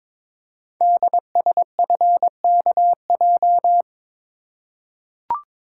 Morse code, heard as a keyed tone, about 23 wpm, 700 Hz. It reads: DSFKJ
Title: Code Groups - 5 Letters - 22wpm